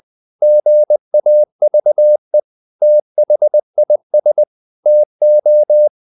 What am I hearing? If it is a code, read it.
Morse code, heard as a keyed tone, about 20 wpm, 600 Hz. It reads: GAVE THIS TO
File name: the old man and the sea / the old man and the sea part 1 20wpm